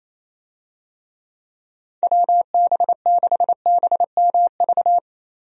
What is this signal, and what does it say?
Morse code, heard as a keyed tone, about 28 wpm, 700 Hz. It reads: WB6BMV
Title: USA Call Signs - 6 Characters - Repeated 2x - 28wpm